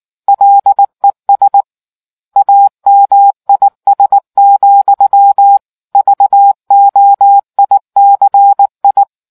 Morse code, heard as a keyed tone, about 18 wpm, 800 Hz. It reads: LES AMIS, VOICI